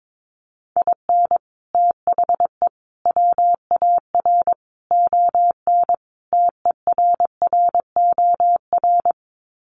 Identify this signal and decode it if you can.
Morse code, heard as a keyed tone, about 22 wpm, 700 Hz. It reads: IN THE WAR ON TERROR
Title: Sets of 5 Words Encore2 22wpm